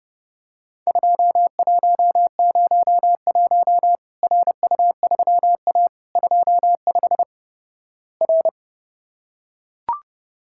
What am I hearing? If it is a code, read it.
Morse code, heard as a keyed tone, about 30 wpm, 700 Hz. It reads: 2101 RU3A 25 R
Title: Worked All Europe DX Contest - QTC Exchange - 30wpm